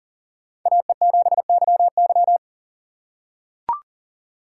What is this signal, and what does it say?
Morse code, heard as a keyed tone, about 40 wpm, 700 Hz. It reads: AE7YY
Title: USA Call Signs - 5 Characters - 3x Character Spacing - 40wpm